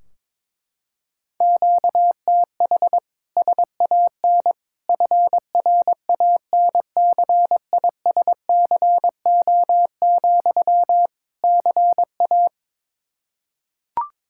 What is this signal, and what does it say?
Morse code, heard as a keyed tone, about 22 wpm, 700 Hz. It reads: QTH SAN FRANCISCO, CA
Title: Practice QSO - Exchange 1 - Beginning - 22wpm